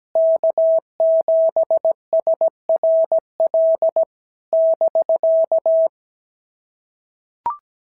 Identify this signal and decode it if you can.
Morse code, heard as a keyed tone, about 15 wpm, 650 Hz. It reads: K7SRL <BK>
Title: SOTA QSO Practice 17wpm